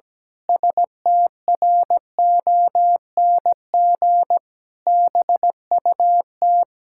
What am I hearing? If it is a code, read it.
Morse code, heard as a keyed tone, about 17 wpm, 700 Hz. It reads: STRONG BUT
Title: Pangram Challenge v2 - 17wpm